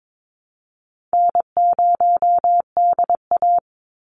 Morse code, heard as a keyed tone, about 22 wpm, 700 Hz. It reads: N0DA
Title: NA SOTA Chaser Callsigns 22wpm